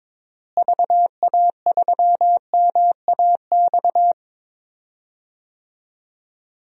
Morse code, heard as a keyed tone, about 22 wpm, 700 Hz. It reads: VA3MAX
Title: DXCC-100 Call Signs - 6 Characters - Repeated 2x - 22wpm